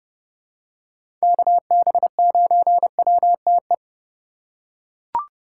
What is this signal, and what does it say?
Morse code, heard as a keyed tone, about 30 wpm, 700 Hz. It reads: KB9WTE
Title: USA Call Signs - 6 Characters - 4x Character Spacing - 30wpm